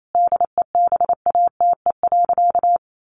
Morse code, heard as a keyed tone, about 28 wpm, 700 Hz. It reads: DEBATE.
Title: Sentences from Top 1100 Words 28wpm